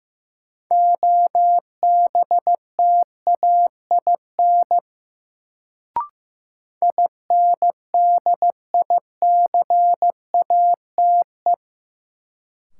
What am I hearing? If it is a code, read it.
Morse code, heard as a keyed tone, about 15 wpm, 700 Hz. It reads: OBTAIN INDICATE
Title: Sentences from Top 1300 Words - New Word Drill Alt - 15wpm